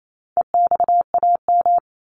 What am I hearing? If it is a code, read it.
Morse code, heard as a keyed tone, about 28 wpm, 700 Hz. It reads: EXAM
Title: Top 501-750 Words 28wpm